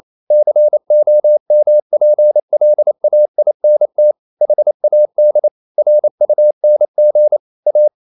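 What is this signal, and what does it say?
Morse code, heard as a keyed tone, about 28 wpm, 600 Hz. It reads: COMPLAINT HAD RUNG A
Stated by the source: Morse code practice at 28 wpm, text of Ken Follett's novel Never in English, from CW med Gnister.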